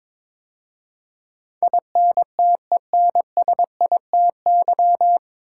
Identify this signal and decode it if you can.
Morse code, heard as a keyed tone, about 22 wpm, 700 Hz. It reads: INTENSITY
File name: 9 Letter Words - First 3 Letters Given - 22wpm